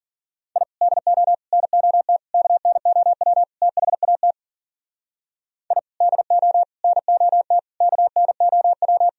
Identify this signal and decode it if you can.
Morse code, heard as a keyed tone, about 52 wpm, 700 Hz. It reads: I DO NOT KNOW THAT I DO NOT KNOW